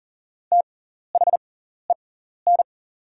Morse code, heard as a keyed tone, about 40 wpm, spaced as for 14 wpm, 700 Hz. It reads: THEN
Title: 6x Character Spacing - Top 100 Words - 40wpm